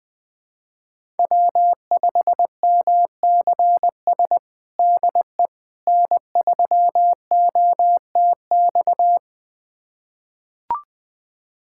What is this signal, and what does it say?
Morse code, heard as a keyed tone, about 21 wpm, 700 Hz. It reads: W5MCS DE N3OTX
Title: USA Callsign DE Callsign - 5 Characters - 20wpm